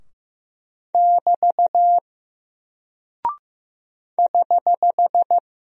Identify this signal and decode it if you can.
Morse code, heard as a keyed tone, about 15 wpm, 700 Hz. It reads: <BT> <HH>
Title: Single Character (Letter:Number:Punctuation:Prosign) Alt 15wpm